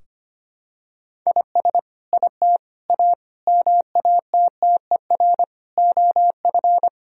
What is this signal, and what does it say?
Morse code, heard as a keyed tone, about 25 wpm, 700 Hz. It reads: IS IT A MATTER OF
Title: Sentences from Top 1000 Words - Review Part 3 - 25wpm